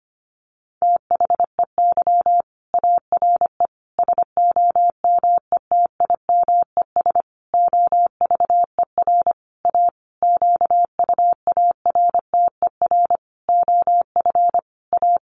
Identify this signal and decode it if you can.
Morse code, heard as a keyed tone, about 25 wpm, 700 Hz. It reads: THEY ARE SOMETIMES OVER A QUARTER OF A
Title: Sentences from Top 1300 Words 25wpm